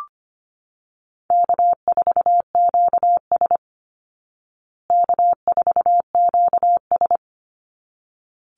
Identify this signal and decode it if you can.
Morse code, heard as a keyed tone, about 25 wpm, 700 Hz. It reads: K4QS K4QS
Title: Call Signs Repeated 3x - 25wpm